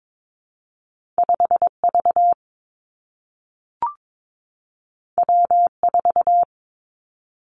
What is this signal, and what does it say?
Morse code, heard as a keyed tone, about 22 wpm, 700 Hz. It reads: 5 V W 4
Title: Two Challenging Letter-Number Combinations 22wpm